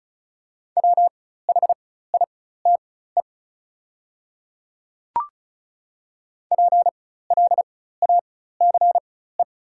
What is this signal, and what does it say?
Morse code, heard as a keyed tone, about 36 wpm, spaced as for 16 wpm, 700 Hz. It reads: WHITE PLACE
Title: 4x Character Spacing - Top 300 Words - 35wpm